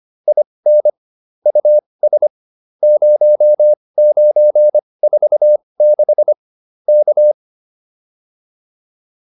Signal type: Morse code

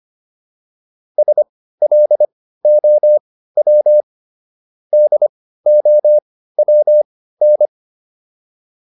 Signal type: Morse code

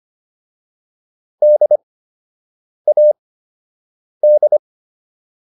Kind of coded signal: Morse code

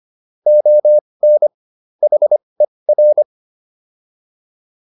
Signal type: Morse code

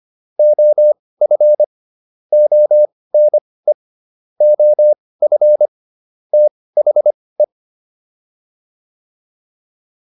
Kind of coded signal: Morse code

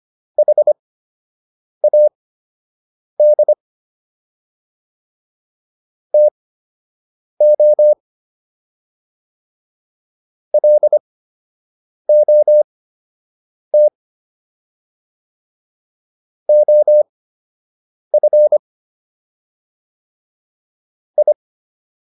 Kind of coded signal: Morse code

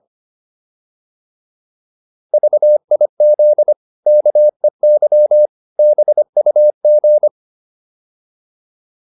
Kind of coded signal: Morse code

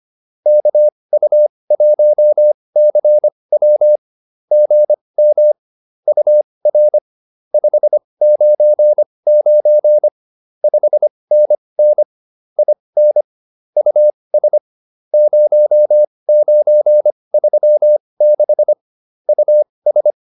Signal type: Morse code

